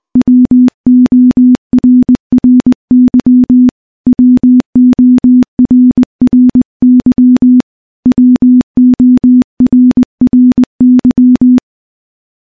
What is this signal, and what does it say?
Morse code, heard as a keyed tone, about 20 wpm, 250 Hz. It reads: WORRY WORRY WORRY
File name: morse-training-20WPM